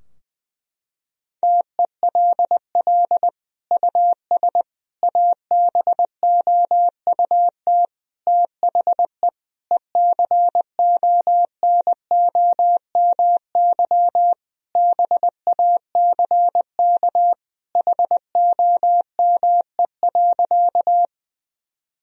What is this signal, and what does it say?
Morse code, heard as a keyed tone, about 21 wpm, 700 Hz. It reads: TELL US ABOUT THE ECONOMY BACK HOME.
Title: Sentences from Top 1000 Words - Review Part 3 - 20wpm